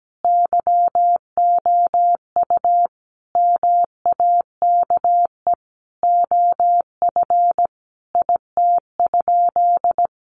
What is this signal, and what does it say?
Morse code, heard as a keyed tone, about 17 wpm, 700 Hz. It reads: YOU MAKE OF IT?
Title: Sentences from Top 100 Words 17wpm